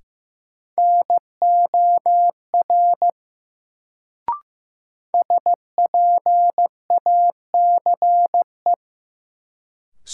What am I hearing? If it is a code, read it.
Morse code, heard as a keyed tone, about 15 wpm, 700 Hz. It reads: NOR SPACE
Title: Sentences from Top 1500 Words - New Word Drill Alt - 15wpm